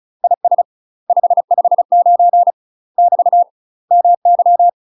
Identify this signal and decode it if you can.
Morse code, heard as a keyed tone, about 36 wpm, 700 Hz. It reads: IS 559 <BT> MY
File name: UK MASTER QSOs TRAINING / 35WPM / UK MASTER QSOs 0013